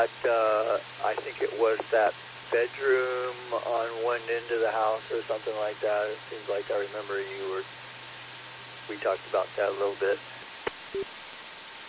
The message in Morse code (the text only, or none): none